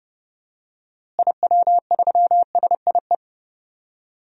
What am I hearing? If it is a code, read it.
Morse code, heard as a keyed tone, about 30 wpm, 700 Hz. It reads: IW3SIE
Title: DXCC-100 Call Signs - 6 Characters - Repeated 3x - 30wpm